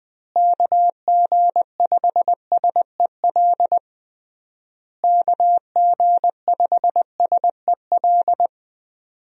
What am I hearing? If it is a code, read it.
Morse code, heard as a keyed tone, about 20 wpm, 700 Hz. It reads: KG5SEL KG5SEL
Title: USA Call Signs - 6 Characters - Repeated 3x - 20wpm